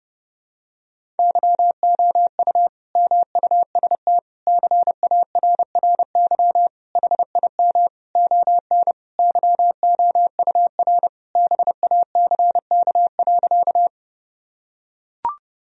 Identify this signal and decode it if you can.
Morse code, heard as a keyed tone, about 32 wpm, 700 Hz. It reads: YOU MUST CARRY HIM ON YOUR BACK.
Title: Sentences from Top 1000 Words - Review Part 1 - 30wpm